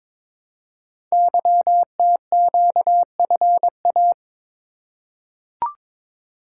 Morse code, heard as a keyed tone, about 22 wpm, 700 Hz. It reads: YTQFA
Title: Code Groups - 5 Letters - 22wpm